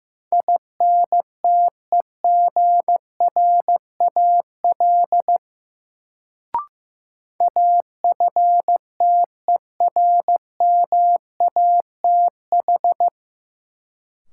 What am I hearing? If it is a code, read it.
Morse code, heard as a keyed tone, about 15 wpm, 700 Hz. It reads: INTEGRAL AFTERMATH
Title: Sets of 5 Words - New Word Drill - 15wpm